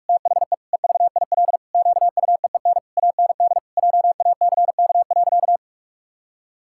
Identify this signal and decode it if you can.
Morse code, heard as a keyed tone, about 46 wpm, 700 Hz. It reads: THE EVIL QUEEN AND JACK.
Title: Pangram Challenge v2 - 45wpm